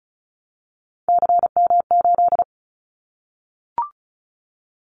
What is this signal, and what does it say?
Morse code, heard as a keyed tone, about 36 wpm, 700 Hz. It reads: CM8
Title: Callsign Prefixes for Top 100 DXCC - 35wpm